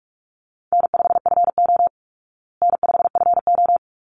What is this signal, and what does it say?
Morse code, heard as a keyed tone, about 44 wpm, 700 Hz. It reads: N5FO N5FO
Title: Easy Call Signs Repeated 2x - 45wpm